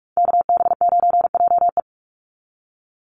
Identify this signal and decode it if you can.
Morse code, heard as a keyed tone, about 44 wpm, 700 Hz. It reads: KB9JE
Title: USA Call Signs 45wpm